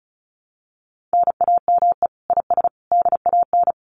Morse code, heard as a keyed tone, about 34 wpm, 700 Hz. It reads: NAME IS DAN
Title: Common QSO Phrases 35wpm